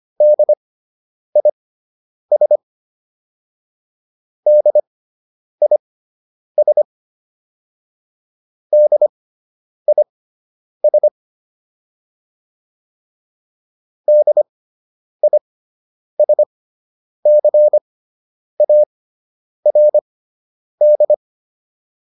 Morse code, heard as a keyed tone, about 25 wpm, spaced as for 9 wpm, 600 Hz. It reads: DIS DIS DIS DISCARD